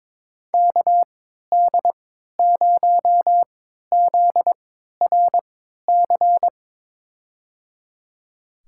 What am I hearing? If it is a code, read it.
Morse code, heard as a keyed tone, about 21 wpm, spaced as for 12 wpm, 700 Hz. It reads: KD0ZRC
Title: USA Call Signs - 3x Character Spacing - 22wpm